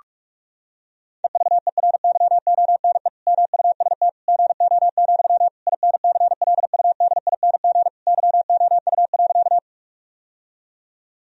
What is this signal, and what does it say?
Morse code, heard as a keyed tone, about 46 wpm, 700 Hz. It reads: EVERYONE MUST GO, INCLUDING YOU.
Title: Sentences from Top 1000 Words - Review Part 4 - 45wpm